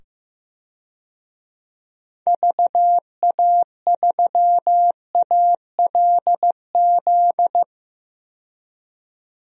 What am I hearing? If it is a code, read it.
Morse code, heard as a keyed tone, about 15 wpm, 700 Hz. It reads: VA3ALZ
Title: DXCC-100 Call Signs - 6 Characters - 15wpm